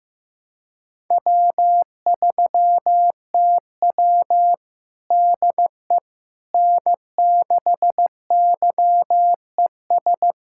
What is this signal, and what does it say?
Morse code, heard as a keyed tone, about 15 wpm, 700 Hz. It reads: W3TW DE N6YES
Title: Practice QSO - Exchange 4 - Ending - 15wpm